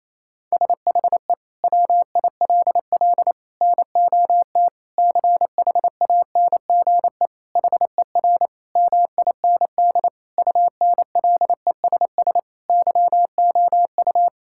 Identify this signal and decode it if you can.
Morse code, heard as a keyed tone, about 28 wpm, 700 Hz. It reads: SHE WILL NOT CHANGE HER MIND UNLESS YOU
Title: Sentences from Top 1000 Words - Review Part 6 - 28wpm